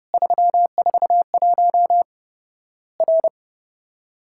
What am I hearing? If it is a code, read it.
Morse code, heard as a keyed tone, about 32 wpm, 700 Hz. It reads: 341 R
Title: Worked All Europe DX Contest - QTC Exchange - 30wpm